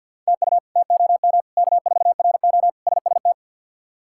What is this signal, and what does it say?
Morse code, heard as a keyed tone, about 46 wpm, 700 Hz. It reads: TU TOM K4RO SST
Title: K1USN SST QSO Practice - 50wpm